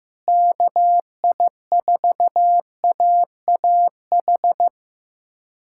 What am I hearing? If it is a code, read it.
Morse code, heard as a keyed tone, about 15 wpm, 700 Hz. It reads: KI4AAH
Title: USA Callsign DE Callsign - 6 Characters - 15wpm